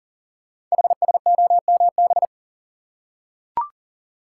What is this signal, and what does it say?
Morse code, heard as a keyed tone, about 40 wpm, 690 Hz. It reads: HSOMB
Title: Code Groups - 5 Letters - 40wpm